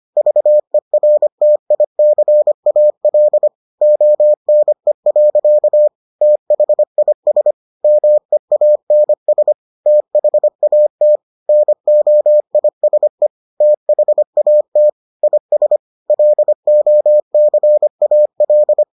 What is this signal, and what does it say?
Morse code, heard as a keyed tone, about 25 wpm, 600 Hz. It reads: VERTICAL ONE. THIS MEANS THAT NOISE THAT IS LOCAL